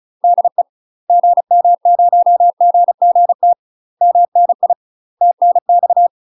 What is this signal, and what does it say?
Morse code, heard as a keyed tone, about 33 wpm, 700 Hz. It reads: DE GM0GGT MNI TNX
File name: UK MASTER QSOs TRAINING / 35WPM / UK MASTER QSOs 0028